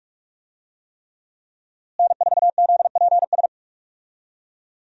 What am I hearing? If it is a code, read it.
Morse code, heard as a keyed tone, about 46 wpm, 700 Hz. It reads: N4ZPS
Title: USA Call Signs - 5 Characters - Repeated 3x - 45wpm